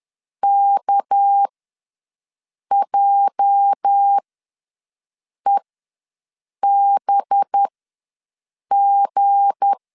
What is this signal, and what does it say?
Morse code, heard as a keyed tone, about 10 wpm, spaced as for 5 wpm, 800 Hz. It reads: KJEBG
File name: LEZIONI / 8 YJ45 / 8 YJ ws 10wpm 4043